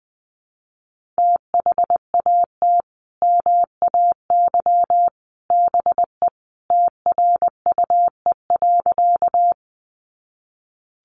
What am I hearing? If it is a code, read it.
Morse code, heard as a keyed tone, about 19 wpm, 700 Hz. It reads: THAT MAY BE TRUE.
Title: Sentences from Top 400 Words 20wpm